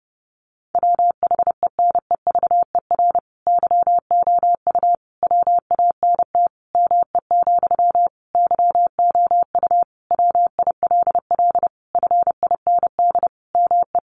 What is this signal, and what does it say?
Morse code, heard as a keyed tone, about 30 wpm, 700 Hz. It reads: WHENEVER YOU WANT ME, YOU WILL FIND ME
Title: Sentences from Top 1200 Words 30wpm